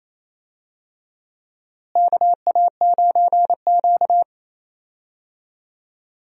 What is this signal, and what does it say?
Morse code, heard as a keyed tone, about 28 wpm, 700 Hz. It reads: KA9Q
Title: USA Call Signs - 4 Characters - Repeated 2x - 28wpm